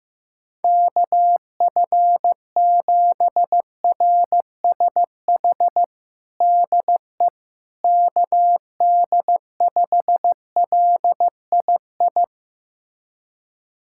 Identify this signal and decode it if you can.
Morse code, heard as a keyed tone, about 15 wpm, 700 Hz. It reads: KF7RSH DE KD5LII
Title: USA Callsign DE Callsign 15wpm